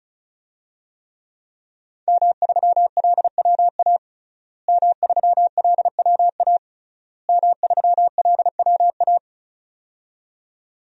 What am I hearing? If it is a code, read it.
Morse code, heard as a keyed tone, about 36 wpm, 700 Hz. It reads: M3LWA M3LWA M3LWA
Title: DXCC-100 Call Signs - 5 Characters - Repeated 3x - 35wpm